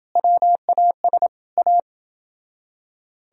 Morse code, heard as a keyed tone, about 29 wpm, 700 Hz. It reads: WAS A